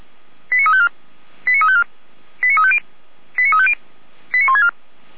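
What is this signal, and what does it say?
Five-tone selective call (ZVEI), heard as quick runs of five tones, tones of about 70 ms. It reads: 89354 89354 89350 89350 89164
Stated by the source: ZVEI five-tone selective call received by SDR, address 89504.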